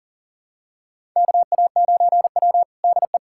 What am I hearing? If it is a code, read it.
Morse code, heard as a keyed tone, about 40 wpm, 700 Hz. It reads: KA9W DE
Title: USA Callsign DE Callsign - 4 Characters - 40wpm